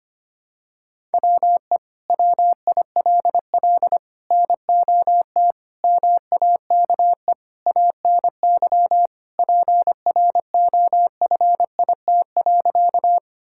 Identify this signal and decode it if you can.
Morse code, heard as a keyed tone, about 25 wpm, 700 Hz. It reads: WE WILL NOT MAKE ANY PROFIT.
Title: Sentences from Top 1000 Words - Review Part 5 - 25wpm